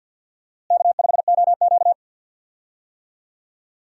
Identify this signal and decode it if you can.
Morse code, heard as a keyed tone, about 52 wpm, 700 Hz. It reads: K5OQ
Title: USA Call Signs - 4 Characters - Repeated 2x - 50wpm